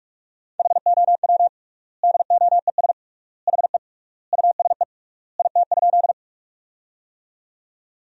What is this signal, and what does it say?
Morse code, heard as a keyed tone, about 41 wpm, 700 Hz. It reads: HOW DOES HE USE IT?
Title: 3x Word-Spacing - Sentences from Top 200 Words - 45wpm